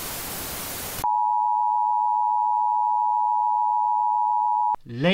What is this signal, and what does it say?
Emergency Alert System (SAME) tones (the 853 and 960 Hz attention tone), with speech. It is heard from 1.0 to 4.8 s.